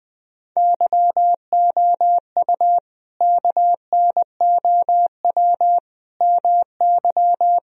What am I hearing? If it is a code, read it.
Morse code, heard as a keyed tone, about 21 wpm, 700 Hz. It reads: YOU KNOW MY
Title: Sentences from Top 1000 Words - Review Part 3 - 20wpm